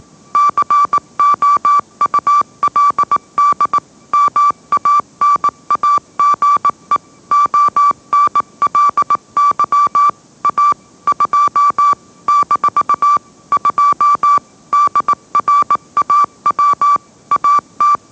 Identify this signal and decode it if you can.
Morse code, heard as a keyed tone, about 18 wpm, 1160 Hz. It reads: COULD MANAGE ONLY A 2 - 2 DRAW AT